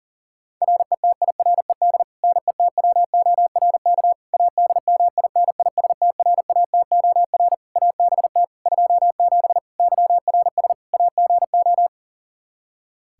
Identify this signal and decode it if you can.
Morse code, heard as a keyed tone, about 40 wpm, 700 Hz. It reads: RETIRED NETWORK ADMINISTRATOR ABT 27 YRS AGO
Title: K1USN SST QSO Practice 40wpm